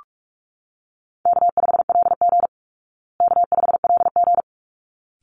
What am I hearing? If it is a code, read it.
Morse code, heard as a keyed tone, about 44 wpm, 700 Hz. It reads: K5LG K5LG
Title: Call Signs Repeated 3x - 45wpm